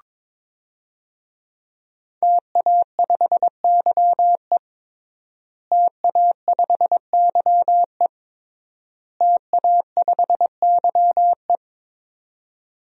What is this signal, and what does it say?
Morse code, heard as a keyed tone, about 22 wpm, 700 Hz. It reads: TA5YE TA5YE TA5YE
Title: DXCC-100 Call Signs - 5 Characters - Repeated 3x - 22wpm